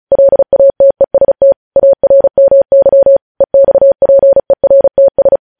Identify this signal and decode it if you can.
Morse code, heard as a keyed tone, about 36 wpm, 550 Hz. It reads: LATEST ARMY EXPERTS
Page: news headlines in morse code at 35 wpm